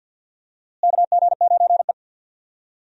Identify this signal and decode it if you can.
Morse code, heard as a keyed tone, about 50 wpm, 700 Hz. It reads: KG9E
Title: USA Call Signs - 4 Characters - Repeated 2x - 50wpm